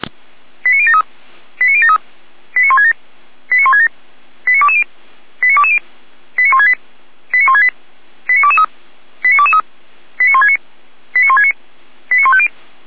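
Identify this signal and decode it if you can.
Five-tone selective call (ZVEI), heard as quick runs of five tones, tones of about 70 ms. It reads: 89073 89073 89167 89167 89220 89220 89168 89168 89223 89223 89169 89169 89150
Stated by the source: ZVEI five-tone selective call 89123 received on an SDR.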